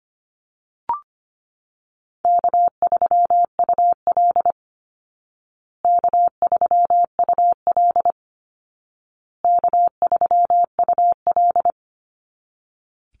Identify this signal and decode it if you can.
Morse code, heard as a keyed tone, about 25 wpm, 700 Hz. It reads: K3UL K3UL K3UL
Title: Call Signs Repeated 2x - 25wpm